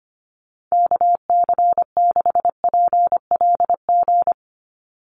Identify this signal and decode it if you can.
Morse code, heard as a keyed tone, about 25 wpm, 700 Hz. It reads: KC6PLG
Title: USA Call Signs 25wpm